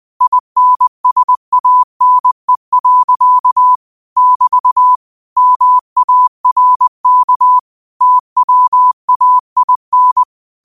Morse code, heard as a keyed tone, about 20 wpm, 1000 Hz. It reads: INSANE. <BT> MARK TWAIN
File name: qotdfiles / 20wpm